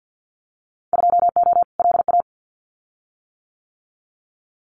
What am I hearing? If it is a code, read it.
Morse code, heard as a keyed tone, about 50 wpm, 700 Hz. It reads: 2O LA
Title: Winter Field Day Exchanges 50wpm